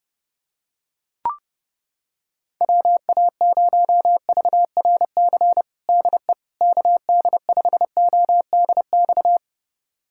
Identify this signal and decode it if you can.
Morse code, heard as a keyed tone, about 32 wpm, 700 Hz. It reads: WA0VRC DE KD5ODX
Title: USA Callsign DE Callsign - 6 Characters - 30wpm